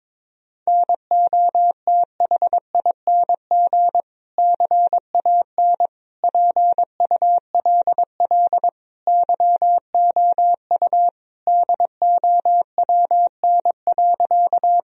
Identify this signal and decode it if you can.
Morse code, heard as a keyed tone, about 22 wpm, 700 Hz. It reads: NOTHING CAN PULL YOU DOWN.
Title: Sentences from Top 1000 Words - Review Part 2 - 22wpm